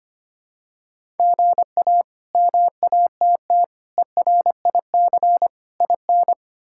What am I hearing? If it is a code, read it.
Morse code, heard as a keyed tone, about 25 wpm, 700 Hz. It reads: GA MATT ERIC IN
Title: K1USN SST QSO Practice - 25wpm